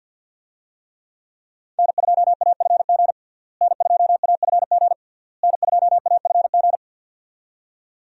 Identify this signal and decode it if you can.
Morse code, heard as a keyed tone, about 50 wpm, 700 Hz. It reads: N2AFG N2AFG N2AFG
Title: USA Call Signs - 5 Characters - Repeated 3x - 50wpm